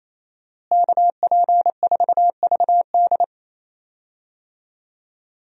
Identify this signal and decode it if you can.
Morse code, heard as a keyed tone, about 28 wpm, 700 Hz. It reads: KP4VD
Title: DXCC-100 Call Signs - 5 Characters - Repeated 3x - 28wpm